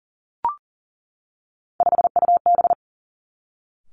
Morse code, heard as a keyed tone, about 40 wpm, 700 Hz. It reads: 5UB
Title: Three Challenging Letter-Number Combinations 40wpm